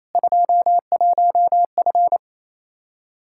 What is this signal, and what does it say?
Morse code, heard as a keyed tone, about 28 wpm, 700 Hz. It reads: 21F
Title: Practice QSO - Exchange 2 - Info - 28wpm